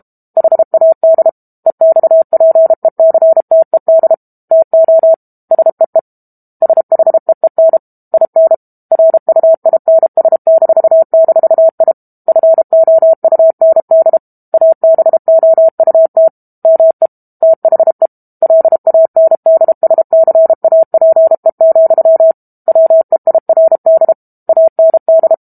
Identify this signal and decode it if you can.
Morse code, heard as a keyed tone, about 32 wpm, 650 Hz. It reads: HAD EXPECTED TO SEE SHEEN IN RUINS--I FOUND ABOUT ME THE LANDSCAPE, WEIRD AND
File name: Book Two / 06-Chapter 6 - 32 WPM